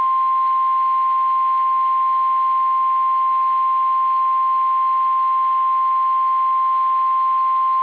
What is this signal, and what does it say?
Emergency Alert System (SAME) tones (the 1050 Hz attention tone). It is heard from 0.3 to 7.0 s.
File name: eas / KSDZKDJL / KSDZKDJL